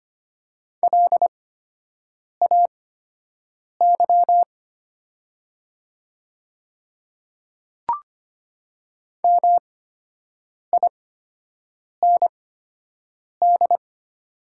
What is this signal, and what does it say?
Morse code, heard as a keyed tone, about 25 wpm, spaced as for 7 wpm, 700 Hz. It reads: LAY MIND
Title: 8x Character Spacing - Top 500 Words - 25wpm